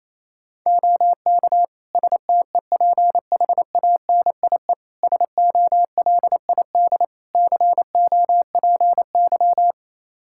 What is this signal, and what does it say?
Morse code, heard as a keyed tone, about 28 wpm, 700 Hz. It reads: OK STEPHANIE SOLID COPY
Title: Practice QSO - Exchange 3 - More Info - 28wpm